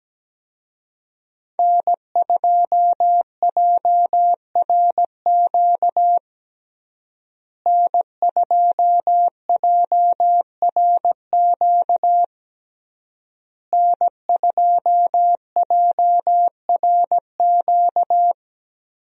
Morse code, heard as a keyed tone, about 17 wpm, 700 Hz. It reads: N2JRQ N2JRQ N2JRQ
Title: USA Call Signs - 5 Characters - Repeated 3x - 17wpm